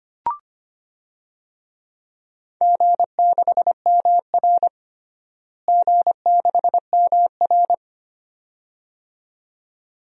Morse code, heard as a keyed tone, about 25 wpm, 700 Hz. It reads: G6MR G6MR
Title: DXCC-100 Call Signs - 4 Characters - Repeated 2x - 25wpm